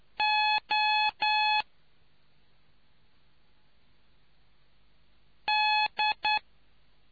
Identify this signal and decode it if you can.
Morse code, heard as a keyed tone, about 9 wpm, 850 Hz. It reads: O D